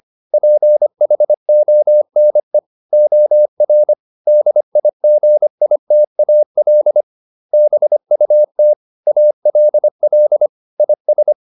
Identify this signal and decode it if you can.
Morse code, heard as a keyed tone, about 25 wpm, 600 Hz. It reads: PHONE OR DIGITAL BUT ALL IS